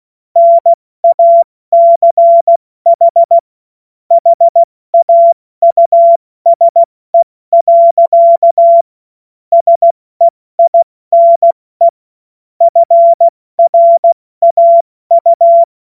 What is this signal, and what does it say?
Morse code, heard as a keyed tone, about 16 wpm, 680 Hz. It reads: NACH HAUSE. SEINE FRAU
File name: Morsekurs / Witze / Witz 06. April 2021, 70 F80